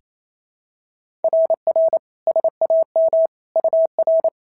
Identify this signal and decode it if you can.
Morse code, heard as a keyed tone, about 25 wpm, 650 Hz. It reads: RR SAM UR
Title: SOTA QSO Practice 28wpm